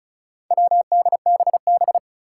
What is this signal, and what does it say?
Morse code, heard as a keyed tone, about 36 wpm, 700 Hz. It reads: WDBB
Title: Four Challenging Letter-Number Combinations 35wpm